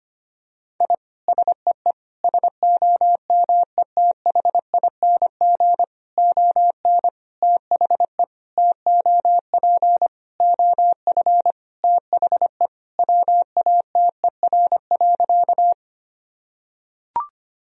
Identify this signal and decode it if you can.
Morse code, heard as a keyed tone, about 25 wpm, 700 Hz. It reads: I SEE SOMETHING ON THE TOP OF THE WATER.
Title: Sentences from Top 1000 Words - Review Part 2 - 25wpm